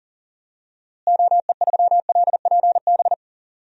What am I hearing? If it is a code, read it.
Morse code, heard as a keyed tone, about 37 wpm, 700 Hz. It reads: OE3LPB
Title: DXCC-100 Call Signs - 6 Characters - Repeated 2x - 40wpm